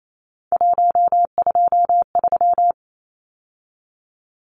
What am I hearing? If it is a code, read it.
Morse code, heard as a keyed tone, about 28 wpm, 700 Hz. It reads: 123
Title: Numbers - 3-Digit 28wpm